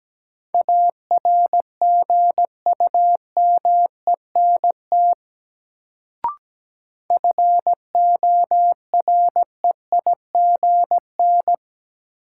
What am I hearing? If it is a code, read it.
Morse code, heard as a keyed tone, about 17 wpm, 700 Hz. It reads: ARGUMENT FOREIGN
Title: Sentences from Top 1400 Words - New Word Drill Alt - 17wpm